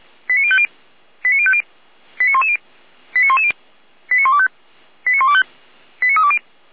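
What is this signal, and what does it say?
Five-tone selective call (ZVEI), heard as quick runs of five tones, tones of about 70 ms. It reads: 89050 89050 89110 89110 89125 89125 89320
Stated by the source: Five-tone ZVEI selective call received on an SDR.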